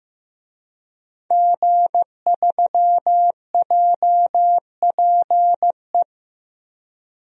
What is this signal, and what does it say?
Morse code, heard as a keyed tone, about 15 wpm, 700 Hz. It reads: G3JPE
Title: DXCC-100 Call Signs - 5 Characters - Repeated 3x - 15wpm